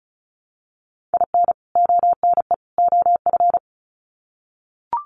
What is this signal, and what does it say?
Morse code, heard as a keyed tone, about 34 wpm, 700 Hz. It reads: IN ONE OF
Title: Sets of 3 Words 35wpm